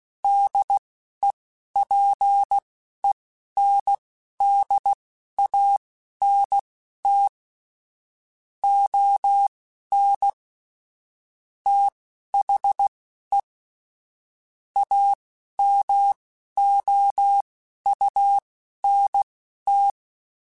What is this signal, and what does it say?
Morse code, heard as a keyed tone, about 16 wpm, spaced as for 11 wpm, 800 Hz. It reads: DEPENDANT ON THE AMOUNT
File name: Technical Subjects / Technical Subjects-mp3s / XL @10